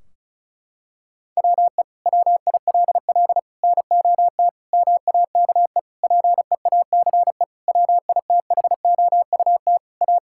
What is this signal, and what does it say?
Morse code, heard as a keyed tone, about 36 wpm, 700 Hz. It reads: WE WILL NOT MAKE PEACE WITHOUT A
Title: Sentences from Top 1600 Words 35wpm